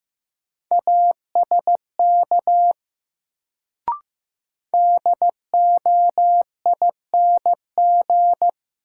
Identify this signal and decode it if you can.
Morse code, heard as a keyed tone, about 15 wpm, 700 Hz. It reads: ASK DOING
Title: Sets of 4 Words - New Word Drill - 15wpm